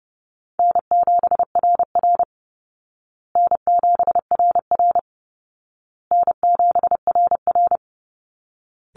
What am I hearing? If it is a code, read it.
Morse code, heard as a keyed tone, about 30 wpm, 700 Hz. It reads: N7RR N7RR N7RR
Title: Easy Call Signs Repeated 3x - 30wpm